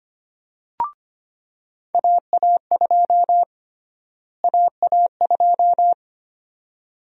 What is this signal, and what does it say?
Morse code, heard as a keyed tone, about 25 wpm, 700 Hz. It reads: AA2 AA2
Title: USA Call Sign Prefixes - Repeated 2x - 25wpm